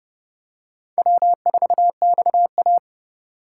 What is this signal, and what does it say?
Morse code, heard as a keyed tone, about 29 wpm, 690 Hz. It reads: W4XA